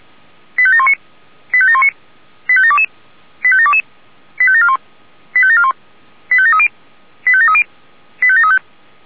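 Five-tone selective call (ZVEI), heard as quick runs of five tones, tones of about 70 ms. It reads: 85719 85719 85722 85722 85741 85741 85730 85730 85735
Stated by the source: Five-tone ZVEI selective call received by SDR.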